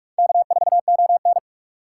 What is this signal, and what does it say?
Morse code, heard as a keyed tone, about 41 wpm, 700 Hz. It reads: K4ON